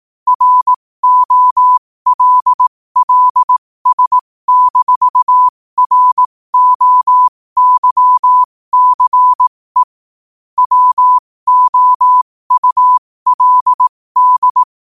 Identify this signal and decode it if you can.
Morse code, heard as a keyed tone, about 18 wpm, spaced as for 15 wpm, 1000 Hz. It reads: ROLLS-ROYCE WOULD
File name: qotdfiles / 15wpm